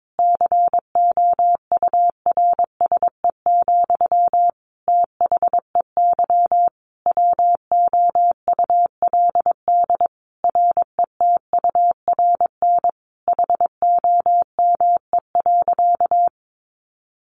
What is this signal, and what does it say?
Morse code, heard as a keyed tone, about 22 wpm, 700 Hz. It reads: COURSE, THEY WOULD RETURN HOME.
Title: Sentences from Top 1400 Words 22wpm